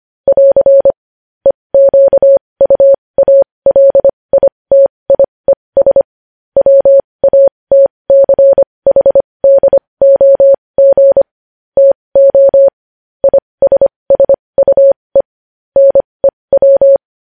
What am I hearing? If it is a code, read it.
Morse code, heard as a keyed tone, about 25 wpm, spaced as for 20 wpm, 550 Hz. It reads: <AR> EQUALITIES WATCHDOG TO ISSUE NEW
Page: News Headlines in Morse Code at 20 WPM